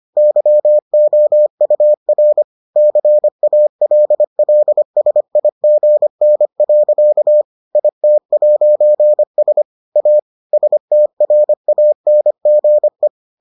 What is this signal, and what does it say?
Morse code, heard as a keyed tone, about 25 wpm, 600 Hz. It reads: YOUR CALLSIGN. IT'S A STRANGE